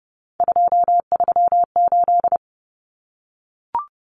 Morse code, heard as a keyed tone, about 30 wpm, 700 Hz. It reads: 238
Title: Common Cut Numbers - 3-Digit 30wpm